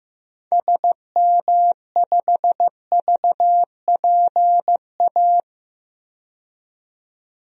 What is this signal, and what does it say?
Morse code, heard as a keyed tone, about 15 wpm, 700 Hz. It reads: SM5VPA
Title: DXCC-100 Call Signs - 6 Characters - 15wpm